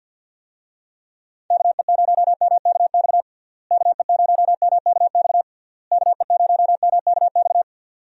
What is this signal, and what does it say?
Morse code, heard as a keyed tone, about 52 wpm, 700 Hz. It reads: KE0MKX KE0MKX KE0MKX
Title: USA Call Signs - 6 Characters - Repeated 3x - 50wpm